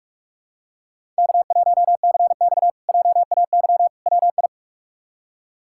Morse code, heard as a keyed tone, about 44 wpm, 700 Hz. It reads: K1CX JAY WI
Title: K1USN SST QSO Practice - 45wpm